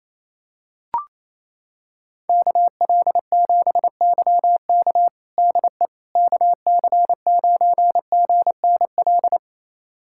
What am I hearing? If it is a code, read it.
Morse code, heard as a keyed tone, about 28 wpm, 700 Hz. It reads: KL7YK DE KC9GNL
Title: USA Callsign DE Callsign 28wpm